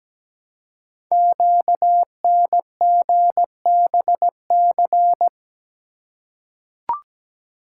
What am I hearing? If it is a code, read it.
Morse code, heard as a keyed tone, about 17 wpm, 700 Hz. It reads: QNGBC
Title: Code Groups - 5 Letters - 17wpm